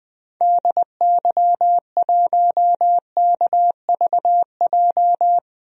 Morse code, heard as a keyed tone, about 21 wpm, 700 Hz. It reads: DY1KVJ
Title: DXCC-100 Call Signs - 6 Characters - 20wpm